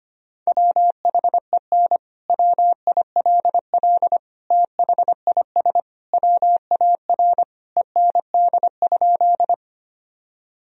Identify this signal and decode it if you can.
Morse code, heard as a keyed tone, about 25 wpm, 700 Hz. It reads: WHEN WILL THIS WAR END?
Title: Sentences from Top 1000 Words - Review Part 2 - 25wpm